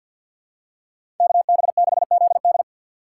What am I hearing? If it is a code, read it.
Morse code, heard as a keyed tone, about 46 wpm, 700 Hz. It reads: KB6ZD
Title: USA Call Signs - 5 Characters - 3x Character Spacing - 50wpm